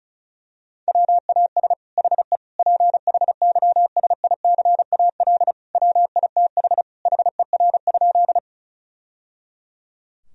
Morse code, heard as a keyed tone, about 34 wpm, 700 Hz. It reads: WAS HE PHYSICAL WITH HER?
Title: Sentences from Top 1000 Words - Review Part 4 - 35wpm